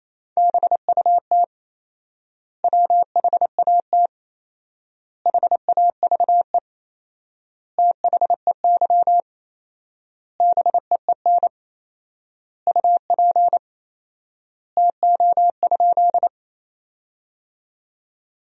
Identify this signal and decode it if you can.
Morse code, heard as a keyed tone, about 28 wpm, 700 Hz. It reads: BUT WHAT HAVE THEY BEEN UP TO?
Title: 4x Word-Spacing - Sentences from Top 100 Words - 28wpm